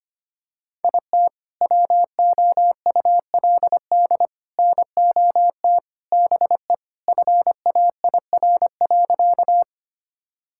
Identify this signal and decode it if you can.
Morse code, heard as a keyed tone, about 25 wpm, 700 Hz. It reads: IT WOULD NOT BE FAIR.
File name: practice / 25wpm / Sentences with 3-Word Phrases - 25wpm